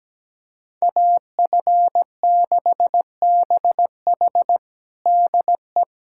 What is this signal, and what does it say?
Morse code, heard as a keyed tone, about 17 wpm, 700 Hz. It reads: AF6BH DE
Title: USA Callsign DE Callsign - 5 Characters - 17wpm